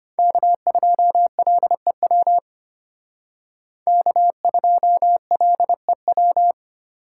Morse code, heard as a keyed tone, about 32 wpm, 700 Hz. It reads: K2LEW K2LEW